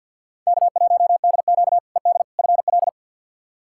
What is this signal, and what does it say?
Morse code, heard as a keyed tone, about 45 wpm, 700 Hz. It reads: K1DQ ED FL
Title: K1USN SST QSO Practice - 50wpm